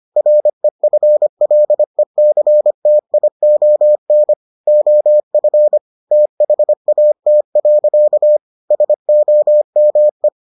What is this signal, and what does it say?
Morse code, heard as a keyed tone, about 25 wpm, 600 Hz. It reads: REFLECTION OF THAT. SOME